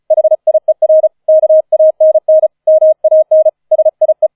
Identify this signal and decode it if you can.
Morse code, heard as a keyed tone, about 34 wpm, 630 Hz. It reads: HIER KANN MAN SIE